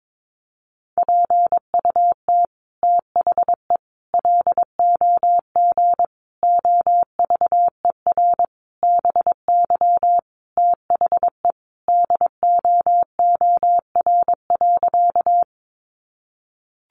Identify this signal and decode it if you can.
Morse code, heard as a keyed tone, about 22 wpm, 700 Hz. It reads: PUT THE LOG OVER BY THE DOOR.
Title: Sentences from Top 1400 Words 22wpm